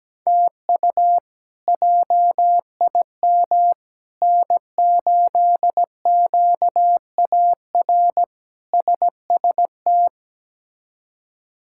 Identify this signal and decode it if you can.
Morse code, heard as a keyed tone, about 17 wpm, 700 Hz. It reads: TU JIM N8QAR SST
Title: K1USN SST QSO Practice - 17wpm